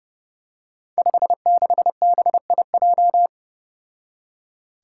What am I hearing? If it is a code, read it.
Morse code, heard as a keyed tone, about 31 wpm, 700 Hz. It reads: 56BIJ
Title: Five Challenging Letter-Number Combinations 30wpm